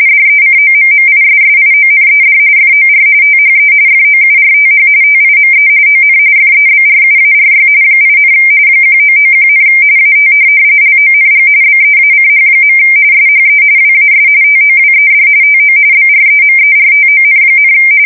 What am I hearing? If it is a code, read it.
RTTY, about 45.45 baud. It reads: DECOMMISSIONED FOR THE FOURTH TIME ON FEBRUARY 8, 1991, AT LONG BEACH. SHE WAS TOWED TO BREMERTON,
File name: fsk170 / USS-New-Jersey